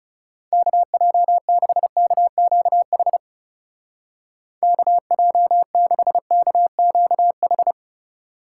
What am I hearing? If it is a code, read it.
Morse code, heard as a keyed tone, about 38 wpm, 700 Hz. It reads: KJ6KQH KJ6KQH